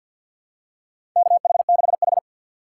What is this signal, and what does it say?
Morse code, heard as a keyed tone, about 46 wpm, 700 Hz. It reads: KH6H
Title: USA Call Signs - 2x Character Spacing - 50wpm